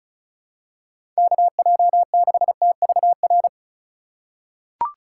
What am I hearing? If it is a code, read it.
Morse code, heard as a keyed tone, about 36 wpm, 700 Hz. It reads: KJ6TVR
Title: USA Call Signs - 6 Characters - 4x Character Spacing - 35wpm